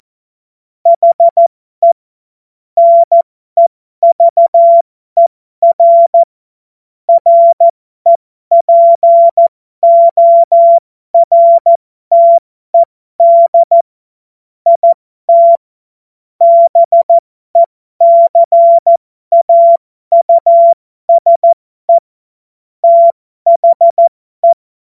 Morse code, heard as a keyed tone, about 14 wpm, spaced as for 12 wpm, 680 Hz. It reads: HE NEVER REPORTED IT BECAUSE THE
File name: Morsekurs / Witze